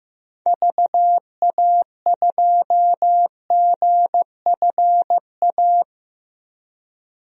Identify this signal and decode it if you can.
Morse code, heard as a keyed tone, about 16 wpm, 700 Hz. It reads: VA2GFA
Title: DXCC-100 Call Signs - 6 Characters - 15wpm